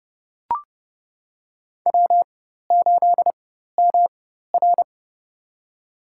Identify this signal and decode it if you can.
Morse code, heard as a keyed tone, about 32 wpm, spaced as for 14 wpm, 700 Hz. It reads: W8MR
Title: USA Call Signs - 4 Characters - 4x Character Spacing - 30wpm